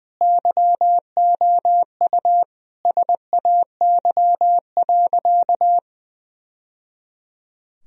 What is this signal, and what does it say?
Morse code, heard as a keyed tone, about 21 wpm, 700 Hz. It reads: YOU SAY.
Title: Sentences from Top 1000 Words - Review Part 1 - 20wpm